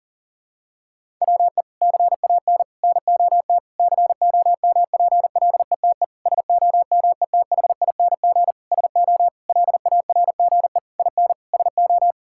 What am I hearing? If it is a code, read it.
Morse code, heard as a keyed tone, about 40 wpm, 700 Hz. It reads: WE CAN NOT COMPLETE SOMETHING SO LARGE IN SO
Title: Sentences from Top 1000 Words - Review Part 6 - 40wpm